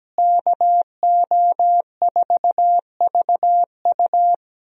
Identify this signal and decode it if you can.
Morse code, heard as a keyed tone, about 17 wpm, 700 Hz. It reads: KO4VU
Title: USA Call Signs - 5 Characters - Repeated 2x - 17wpm